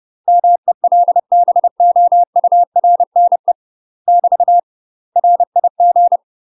Morse code, heard as a keyed tone, about 30 wpm, 700 Hz. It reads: MELBOURNE <BT> RIG
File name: UK MASTER QSOs TRAINING / 30WPM / UK MASTER QSOs 0003